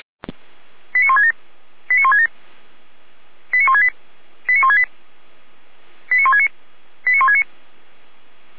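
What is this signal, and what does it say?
Five-tone selective call (ZVEI), heard as quick runs of five tones, tones of about 80 ms. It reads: 89167 89167 89168 89168 89169 89169
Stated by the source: ZVEI five-tone selective call received on an SDR.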